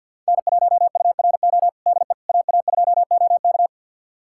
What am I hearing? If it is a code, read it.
Morse code, heard as a keyed tone, about 52 wpm, 700 Hz. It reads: N1URO DE AA2OK
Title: USA Callsign DE Callsign - 5 Characters - 50wpm